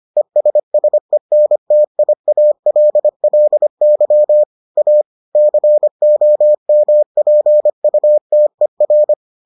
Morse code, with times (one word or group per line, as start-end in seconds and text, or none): 0.2-4.4 ESSENTIALLY
4.8-5.0 A
5.3-9.1 COMPUTER